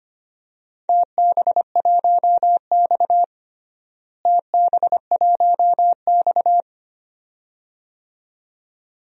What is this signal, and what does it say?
Morse code, heard as a keyed tone, about 25 wpm, 700 Hz. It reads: TB1X TB1X
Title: DXCC-100 Call Signs - 4 Characters - Repeated 2x - 25wpm